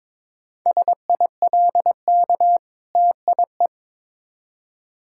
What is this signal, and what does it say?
Morse code, heard as a keyed tone, about 22 wpm, 690 Hz. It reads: SILK TIE
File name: Difficult Words with Hints - 22wpm